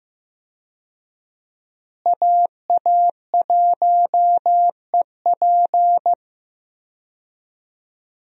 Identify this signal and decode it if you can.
Morse code, heard as a keyed tone, about 15 wpm, 700 Hz. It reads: AA1EP
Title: USA Call Signs - 5 Characters - Repeated 2x - 15wpm